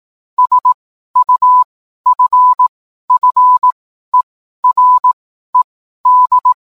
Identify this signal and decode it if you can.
Morse code, heard as a keyed tone, about 18 wpm, spaced as for 12 wpm, 1000 Hz. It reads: SUFFERED
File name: cwblog / cw mp3 directory / mem morse 12